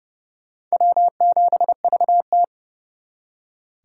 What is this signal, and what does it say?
Morse code, heard as a keyed tone, about 38 wpm, 700 Hz. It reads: W7VT
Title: Speed Racing - USA 1x2 Call Signs 30wpm